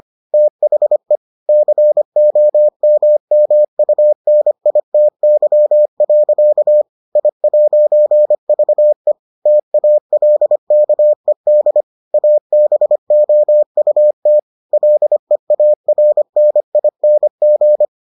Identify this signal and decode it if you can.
Morse code, heard as a keyed tone, about 25 wpm, 600 Hz. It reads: THE COMMUNITY. I'VE TALKED ABOUT LEARNING